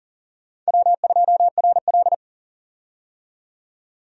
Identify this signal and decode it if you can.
Morse code, heard as a keyed tone, about 40 wpm, 700 Hz. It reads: W2RL
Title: USA Call Signs - 4 Characters - 40wpm